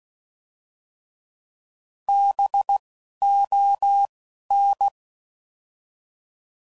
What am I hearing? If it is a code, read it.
Morse code, heard as a keyed tone, about 16 wpm, 800 Hz. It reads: B O N